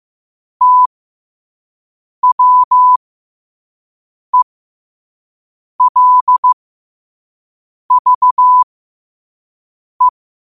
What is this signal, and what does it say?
Morse code, heard as a keyed tone, about 15 wpm, spaced as for 5 wpm, 1000 Hz. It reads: TWELVE
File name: qotdfiles / 5wpm